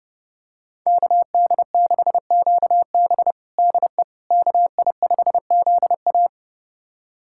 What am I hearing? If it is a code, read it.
Morse code, heard as a keyed tone, about 30 wpm, 700 Hz. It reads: KD6QB DE KI5ZA
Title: USA Callsign DE Callsign - 5 Characters - 30wpm